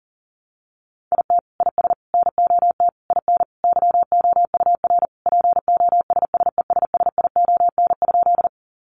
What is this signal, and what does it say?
Morse code, heard as a keyed tone, about 40 wpm, 700 Hz. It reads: IT IS NOT IN YOUR POSSESSION?
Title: Sentences from Top 1400 Words 40wpm